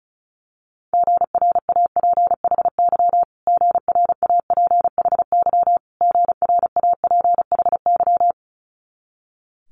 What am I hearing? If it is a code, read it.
Morse code, heard as a keyed tone, about 36 wpm, 700 Hz. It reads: GRAPHY GRAPHY GRAPHY
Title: Multi-letter Phonemes Repeated 3x 35wpm